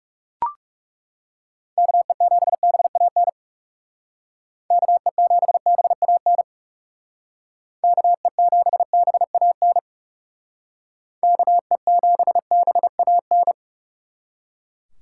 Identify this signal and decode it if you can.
Morse code, heard as a keyed tone, about 37 wpm, 700 Hz. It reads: KE7BAN KE7BAN KE7BAN KE7BAN
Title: Speed Racing - USA 2x3 Call Signs 30wpm